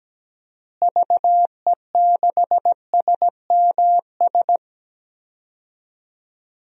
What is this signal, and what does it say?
Morse code, heard as a keyed tone, about 17 wpm, 700 Hz. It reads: VE6SMS
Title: DXCC-100 Call Signs - 6 Characters - Repeated 2x - 17wpm